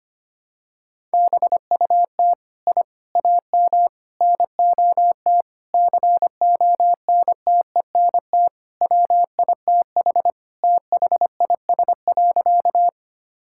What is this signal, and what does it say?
Morse code, heard as a keyed tone, about 25 wpm, 700 Hz. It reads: BUT I AM NOT CONTENT WITH THIS.
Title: Sentences from Top 1000 Words - Review Part 6 - 25wpm